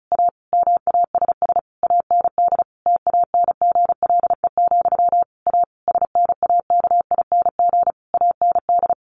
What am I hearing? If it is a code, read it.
Morse code, heard as a keyed tone, about 34 wpm, 700 Hz. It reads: A MASS AND TANGLE, A SNAKING AND
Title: A Sound Of Thunder 35wpm